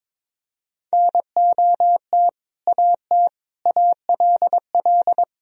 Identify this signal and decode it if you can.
Morse code, heard as a keyed tone, about 22 wpm, 700 Hz. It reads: NOT AT ALL
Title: Sentences from Top 1000 Words - Review Part 4 - 22wpm